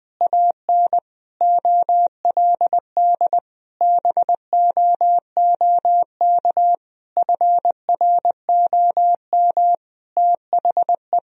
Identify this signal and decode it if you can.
Morse code, heard as a keyed tone, about 21 wpm, 700 Hz. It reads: AN OLD BOOK FROM THE
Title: Sentences from Top 1000 Words - Review Part 4 - 20wpm